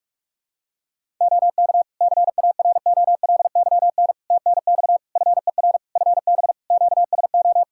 Morse code, heard as a keyed tone, about 43 wpm, 700 Hz. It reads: OK CAROLYN TNX FER FB QSO
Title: Practice QSO - Exchange 4 - Ending - 45wpm